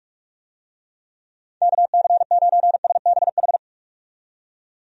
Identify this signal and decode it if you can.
Morse code, heard as a keyed tone, about 45 wpm, 700 Hz. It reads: KC9SBH
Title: USA Call Signs - 6 Characters - 45wpm